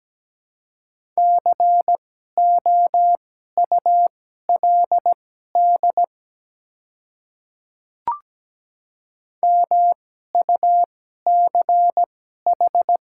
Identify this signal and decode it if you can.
Morse code, heard as a keyed tone, about 17 wpm, spaced as for 12 wpm, 690 Hz. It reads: COULD MUCH
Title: 2x Character Spacing - Top 300 Words - 17wpm